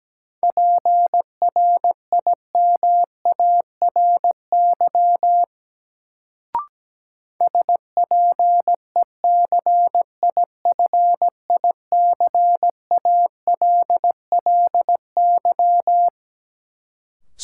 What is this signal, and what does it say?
Morse code, heard as a keyed tone, about 17 wpm, 700 Hz. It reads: PRIMARY SPECIFICALLY
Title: Sentences from Top 1100 Words - New Word Drill Alt - 17wpm